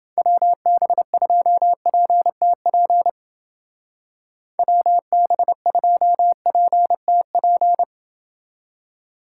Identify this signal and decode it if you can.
Morse code, heard as a keyed tone, about 29 wpm, 700 Hz. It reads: WB2PTP WB2PTP